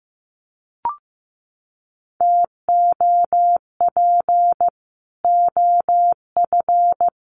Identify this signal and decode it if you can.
Morse code, heard as a keyed tone, about 15 wpm, 700 Hz. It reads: TOP OF
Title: Sets of 3 Words 15wpm